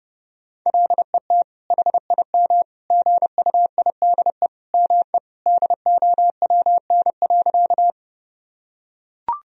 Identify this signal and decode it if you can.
Morse code, heard as a keyed tone, about 30 wpm, 700 Hz. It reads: LET HIM GUIDE ME DOWN.
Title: Sentences from Top 1000 Words - Review Part 3 - 30wpm